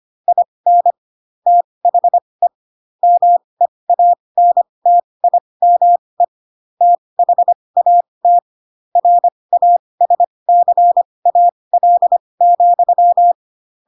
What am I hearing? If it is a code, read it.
Morse code, heard as a keyed tone, about 25 wpm, spaced as for 20 wpm, 700 Hz. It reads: IN THE MEANTIME THAT RASCAL,
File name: Pinocchio / Pinocchio ch4 15 25 ik1hge